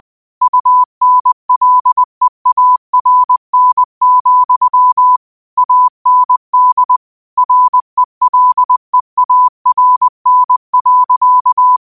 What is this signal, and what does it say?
Morse code, heard as a keyed tone, about 20 wpm, 1000 Hz. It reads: UNLEARN, AND RELEARN.
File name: qotdfiles / 20wpm